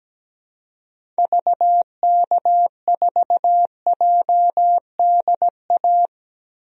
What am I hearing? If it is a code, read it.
Morse code, heard as a keyed tone, about 17 wpm, 700 Hz. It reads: VK4JDA
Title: DXCC-100 Call Signs - 6 Characters - Repeated 3x - 17wpm